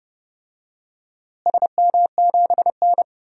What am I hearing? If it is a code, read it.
Morse code, heard as a keyed tone, about 30 wpm, 700 Hz. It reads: SM7N
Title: DXCC-100 Call Signs - 4 Characters - Repeated 2x - 30wpm